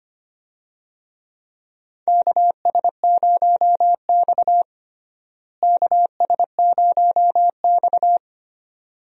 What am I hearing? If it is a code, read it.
Morse code, heard as a keyed tone, about 25 wpm, 700 Hz. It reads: KS0X KS0X
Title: USA Call Signs - 4 Characters - Repeated 2x - 25wpm